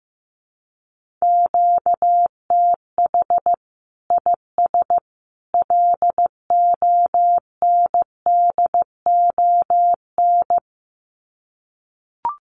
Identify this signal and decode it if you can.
Morse code, heard as a keyed tone, about 15 wpm, 700 Hz. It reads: QTH IS LONDON
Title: Common QSO Phrases 15wpm